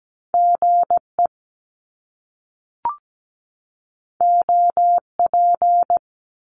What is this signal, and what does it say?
Morse code, heard as a keyed tone, about 17 wpm, 700 Hz. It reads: GE OP
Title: Top Two Letter Combinations in Words 17wpm